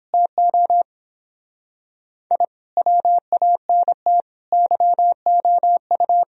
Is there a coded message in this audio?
Yes: Morse code from 0.1 to 6.2 s, heard as a keyed tone, about 30 wpm, 700 Hz. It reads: TO I WANT YOU